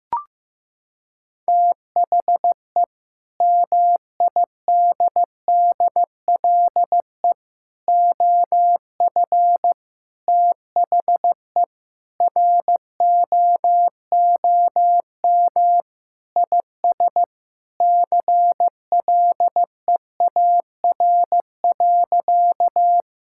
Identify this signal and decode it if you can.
Morse code, heard as a keyed tone, about 15 wpm, 700 Hz. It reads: THE MIDDLE OF THE ROOM IS CLEAR.
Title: Sentences from Top 1000 Words - Review Part 6 - 15wpm